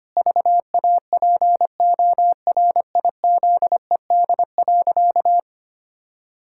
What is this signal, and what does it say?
Morse code, heard as a keyed tone, about 25 wpm, 700 Hz. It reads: VAPORIZED.
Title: Pangram Challenge v2 - 25wpm